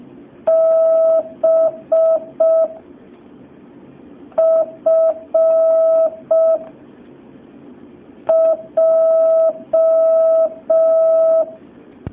Morse code, heard as a keyed tone, about 5 wpm, 650 Hz. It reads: B F J
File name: jh8cbh / 2019cw